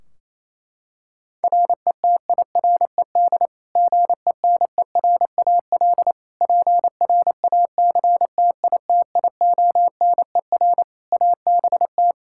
Morse code, heard as a keyed tone, about 28 wpm, 700 Hz. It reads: RETIRED GENERAL PRACTITIONER ABT
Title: Practice QSO - Exchange 3 - More Info - 28wpm